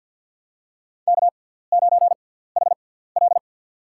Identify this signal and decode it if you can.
Morse code, heard as a keyed tone, about 47 wpm, spaced as for 17 wpm, 700 Hz. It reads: K9HL
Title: USA Call Signs - 4 Characters - 6x Character Spacing - 50wpm